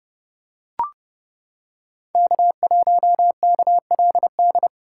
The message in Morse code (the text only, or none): K1KLD